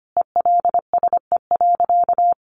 Morse code, heard as a keyed tone, about 25 wpm, 700 Hz. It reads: ELSE.
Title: Sentences from Top 800 Words 25wpm